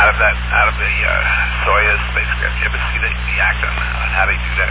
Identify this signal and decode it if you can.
Emergency Alert System (SAME) tones, heard as harsh buzzing bursts, with speech. It is heard from 1.1 to 3.7 s.